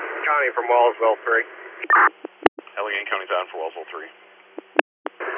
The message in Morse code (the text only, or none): none